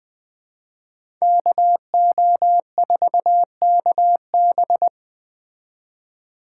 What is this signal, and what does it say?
Morse code, heard as a keyed tone, about 20 wpm, 700 Hz. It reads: KO4KB